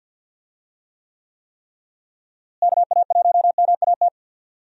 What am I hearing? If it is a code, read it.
Morse code, heard as a keyed tone, about 50 wpm, 700 Hz. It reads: KA1MAT
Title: USA Call Signs - 6 Characters - Repeated 3x - 50wpm